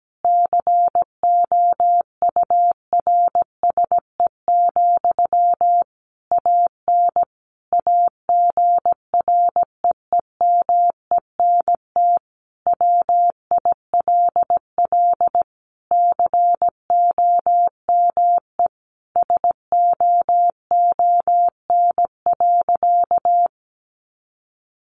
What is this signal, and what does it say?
Morse code, heard as a keyed tone, about 17 wpm, 700 Hz. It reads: COURSE, AN AGREEMENT WILL COME SOON.
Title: Sentences from Top 1500 Words 17wpm